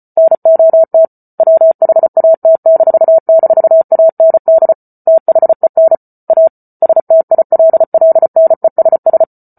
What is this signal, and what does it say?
Morse code, heard as a keyed tone, about 34 wpm, 650 Hz. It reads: NOT WHAT--AND THEN A STILLNESS